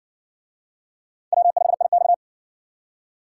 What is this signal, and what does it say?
Morse code, heard as a keyed tone, about 52 wpm, 700 Hz. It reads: W5IX